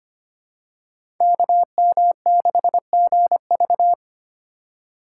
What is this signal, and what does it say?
Morse code, heard as a keyed tone, about 25 wpm, 700 Hz. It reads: KM6GV
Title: USA Call Signs - 5 Characters - 2x Character Spacing - 25wpm